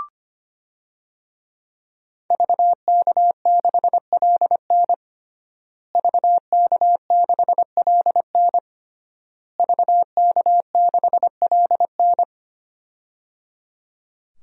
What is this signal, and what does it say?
Morse code, heard as a keyed tone, about 25 wpm, 700 Hz. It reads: VK6LN VK6LN VK6LN
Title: DXCC-100 Call Signs - 5 Characters - Repeated 3x - 25wpm